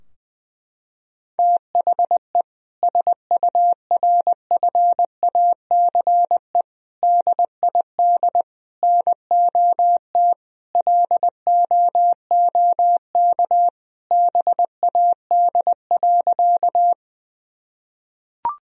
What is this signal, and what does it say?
Morse code, heard as a keyed tone, about 20 wpm, 700 Hz. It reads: THE SURFACE DID NOT LOOK BAD.
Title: Sentences from Top 1000 Words - Review Part 3 - 20wpm